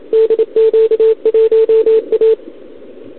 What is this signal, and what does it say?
Morse code, heard as a keyed tone, about 25 wpm, 440 Hz. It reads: DQ1A
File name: CQWWCW2021 / Doderna OL3Z